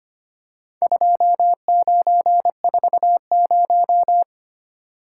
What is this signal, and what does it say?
Morse code, heard as a keyed tone, about 26 wpm, 700 Hz. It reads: 2940
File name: practice / 25wpm / Four Letter-Number Combinations - 25wpm